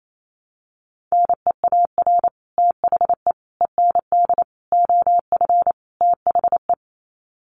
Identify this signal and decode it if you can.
Morse code, heard as a keyed tone, about 28 wpm, 690 Hz. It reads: NEAR THE END OF THE
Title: Sets of 5 Words Encore 28wpm